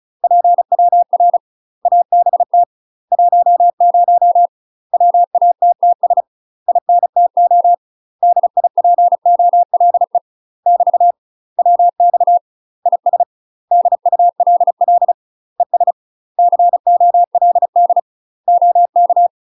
Morse code, heard as a keyed tone, about 35 wpm, 700 Hz. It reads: PWR ABT 10 WATTS INTO DIPOLE <BT> WX IS DULL ES COLD OK
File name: UK MASTER QSOs TRAINING / 35WPM / UK MASTER QSOs 0005